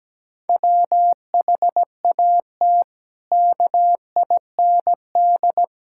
Morse code, heard as a keyed tone, about 17 wpm, 700 Hz. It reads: WHAT KIND
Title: Sentences from Top 1000 Words - Review Part 5 - 17wpm